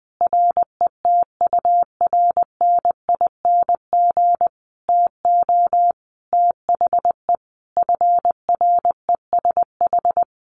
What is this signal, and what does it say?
Morse code, heard as a keyed tone, about 20 wpm, 700 Hz. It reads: RETURNING TO THE FRESH
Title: A Sound Of Thunder 20wpm